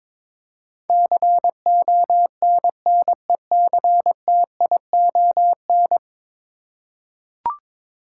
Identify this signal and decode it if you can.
Morse code, heard as a keyed tone, about 22 wpm, 700 Hz. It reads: CONNECTION
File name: practice / 22wpm / 10 Letter Words - First 3 Letters Given - 22wpm